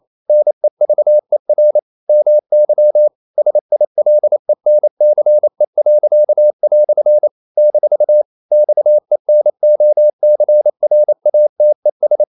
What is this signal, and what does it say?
Morse code, heard as a keyed tone, about 28 wpm, 600 Hz. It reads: NEVER MY SILENCE." - XENOCRATES
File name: CW / Practice / EnglishQuotations0228